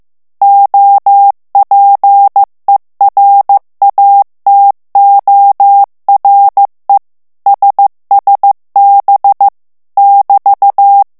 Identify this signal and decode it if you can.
Morse code, heard as a keyed tone, about 15 wpm, 800 Hz. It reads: OPERATORE SSB <BT>
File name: morse mp3 / ITALIEN texte clair 2 vitesse 14 mots minute